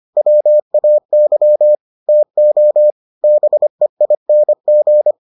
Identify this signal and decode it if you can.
Morse code, heard as a keyed tone, about 25 wpm, 600 Hz. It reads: WAY TO BEING